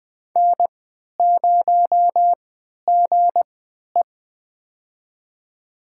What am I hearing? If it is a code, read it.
Morse code, heard as a keyed tone, about 21 wpm, spaced as for 11 wpm, 700 Hz. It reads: N0GE
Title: USA Call Signs - 3x Character Spacing - 20wpm